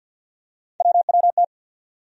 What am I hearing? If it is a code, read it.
Morse code, heard as a keyed tone, about 52 wpm, 700 Hz. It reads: WWT
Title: USA Call Sign Postfixes - 3 Letters 50wpm